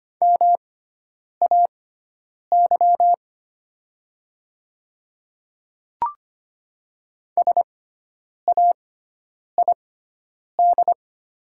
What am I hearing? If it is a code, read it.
Morse code, heard as a keyed tone, about 25 wpm, spaced as for 9 wpm, 700 Hz. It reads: MAY SAID
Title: 6x Character Spacing - Top 100 Words - 25wpm